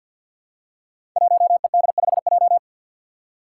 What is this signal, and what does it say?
Morse code, heard as a keyed tone, about 47 wpm, 700 Hz. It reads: 1ED5J
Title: Five Challenging Letter-Number Combinations 50wpm